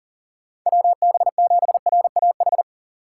Morse code, heard as a keyed tone, about 46 wpm, 700 Hz. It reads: WB7RAH
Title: Speed Racing - USA 2x3 Call Signs 40wpm